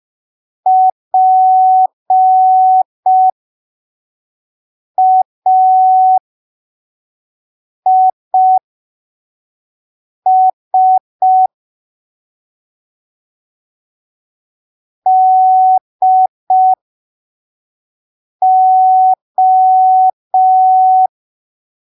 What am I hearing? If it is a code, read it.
Morse code, heard as a keyed tone, about 5 wpm, spaced as for 3 wpm, 750 Hz. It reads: PAIS DO
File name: arquivos / texto / texto 16 05ppm